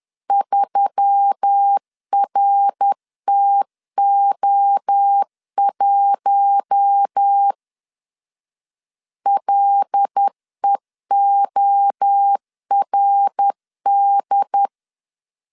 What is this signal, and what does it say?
Morse code, heard as a keyed tone, about 10 wpm, 800 Hz. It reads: 3RTO1 LEORD
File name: CW / DET / 1 6 1500C S B 10wpm 4137